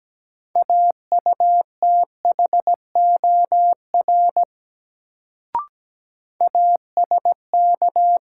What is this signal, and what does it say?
Morse code, heard as a keyed tone, about 17 wpm, 700 Hz. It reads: AUTHOR ASK
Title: Sets of 4 Words - New Word Drill - 17wpm